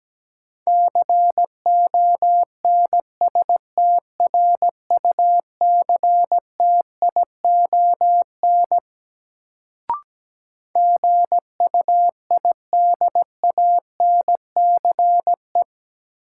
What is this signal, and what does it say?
Morse code, heard as a keyed tone, about 17 wpm, 700 Hz. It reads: CONSTRUCTION GUIDANCE
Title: Sentences from Top 1600 Words - New Word Drill Alt - 17wpm